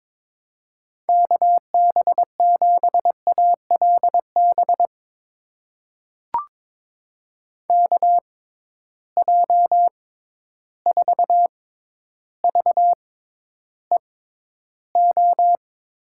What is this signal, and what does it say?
Morse code, heard as a keyed tone, about 22 wpm, spaced as for 11 wpm, 700 Hz. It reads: KB7ALB KJ4VEO
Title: USA Call Signs - 6 Characters - 6x Character Spacing - 22wpm